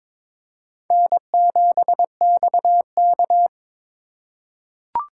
Morse code, heard as a keyed tone, about 23 wpm, 700 Hz. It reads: N7XK
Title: USA Call Signs - 4x Character Spacing - 22wpm